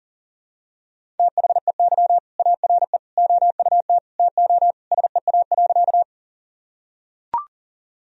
Morse code, heard as a keyed tone, about 40 wpm, 700 Hz. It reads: THEY ARE OUT TO SEA.
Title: Sentences from Top 1000 Words - Review Part 1 - 40wpm